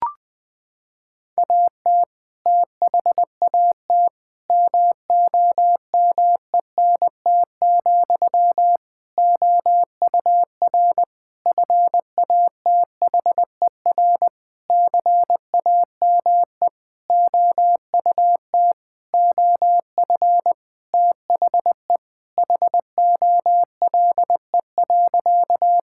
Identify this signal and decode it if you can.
Morse code, heard as a keyed tone, about 20 wpm, 700 Hz. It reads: AT THAT MOMENT, OUR FATHER CAME OUT OF THE HOLE.
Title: Sentences from Top 1600 Words 20wpm